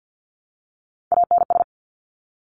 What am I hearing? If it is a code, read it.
Morse code, heard as a keyed tone, about 50 wpm, 700 Hz. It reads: ANS
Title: Top Three Letter Combinations in Words 50wpm